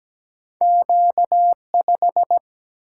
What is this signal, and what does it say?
Morse code, heard as a keyed tone, about 17 wpm, 700 Hz. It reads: Q5
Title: Single-Letter Single-Number 17wpm